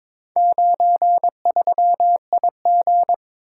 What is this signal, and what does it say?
Morse code, heard as a keyed tone, about 21 wpm, 700 Hz. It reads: 93IG
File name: practice / 22wpm / Four Letter-Number Combinations - 22wpm